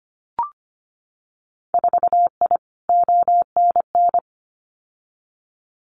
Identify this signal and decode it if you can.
Morse code, heard as a keyed tone, about 25 wpm, 700 Hz. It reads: 4I ONN
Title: Winter Field Day Exchanges Encore 25wpm